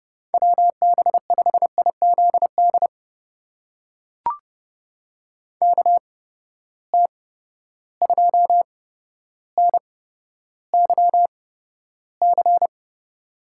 Morse code, heard as a keyed tone, about 30 wpm, spaced as for 13 wpm, 700 Hz. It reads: WB5IZD KT2NYC
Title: USA Call Signs - 6 Characters - 8x Character Spacing - 30wpm